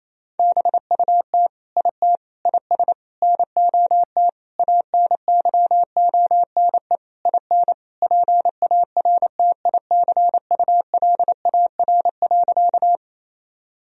Morse code, heard as a keyed tone, about 28 wpm, 700 Hz. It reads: BUT IT IS NOT ANYONE IN PARTICULAR.
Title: Sentences from Top 1000 Words - Review Part 4 - 28wpm